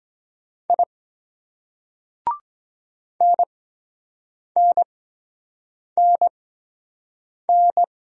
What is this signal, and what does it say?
Morse code, heard as a keyed tone, about 22 wpm, spaced as for 7 wpm, 700 Hz. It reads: I NNNN